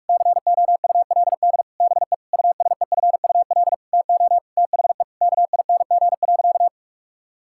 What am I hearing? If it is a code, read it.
Morse code, heard as a keyed tone, about 46 wpm, 700 Hz. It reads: COULD BE USEFUL TO THE KING.
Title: Sentences from Top 1000 Words - Review Part 6 - 45wpm